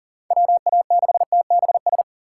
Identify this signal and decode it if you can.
Morse code, heard as a keyed tone, about 40 wpm, 700 Hz. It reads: WA6TBS
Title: USA Call Signs - 6 Characters - 3x Character Spacing - 40wpm